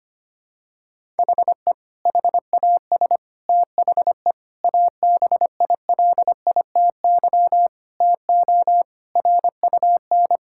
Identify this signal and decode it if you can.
Morse code, heard as a keyed tone, about 25 wpm, 700 Hz. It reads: HE HAS THE ABILITY TO RUN
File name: practice / 25wpm / Sentences with 3-Word Phrases - 25wpm